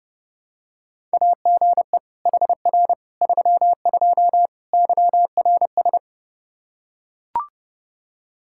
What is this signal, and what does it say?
Morse code, heard as a keyed tone, about 30 wpm, 700 Hz. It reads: AGE HR 32 YRS
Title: Practice QSO - Exchange 3 - More Info - 30wpm